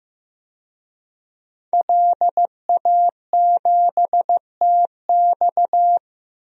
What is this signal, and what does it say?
Morse code, heard as a keyed tone, about 15 wpm, 700 Hz. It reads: LA7TX
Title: DXCC-100 Call Signs - 5 Characters - 15wpm